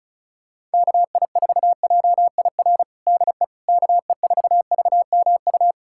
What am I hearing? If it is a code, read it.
Morse code, heard as a keyed tone, about 34 wpm, 700 Hz. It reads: KI4JIR DE KE4VMU
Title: USA Callsign DE Callsign - 6 Characters - 35wpm